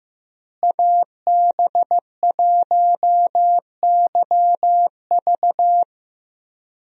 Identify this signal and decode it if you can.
Morse code, heard as a keyed tone, about 15 wpm, 700 Hz. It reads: AB1YV
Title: USA Call Signs - 5 Characters - Repeated 2x - 15wpm